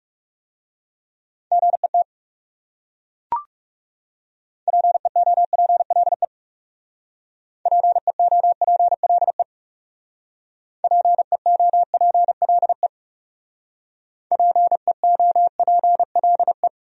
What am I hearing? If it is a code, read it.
Morse code, heard as a keyed tone, about 39 wpm, 700 Hz. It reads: GET PEOPLE PEOPLE PEOPLE PEOPLE